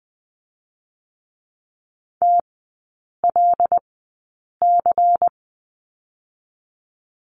Morse code, heard as a keyed tone, about 21 wpm, 700 Hz. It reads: T L C
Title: 145 Lesson 24e - Letter y - Three Characters 20wpm